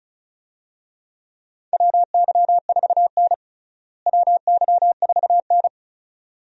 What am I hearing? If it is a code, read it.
Morse code, heard as a keyed tone, about 34 wpm, 700 Hz. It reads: WY4N WY4N
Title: USA Call Signs - 4 Characters - Repeated 2x - 35wpm